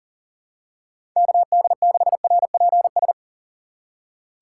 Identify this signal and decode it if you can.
Morse code, heard as a keyed tone, about 40 wpm, 700 Hz. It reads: KD6RPS
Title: USA Call Signs - 4x Character Spacing - 40wpm